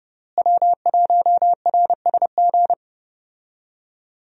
Morse code, heard as a keyed tone, about 29 wpm, 700 Hz. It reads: W1RSG
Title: USA Call Signs - 5 Characters - Repeated 3x - 30wpm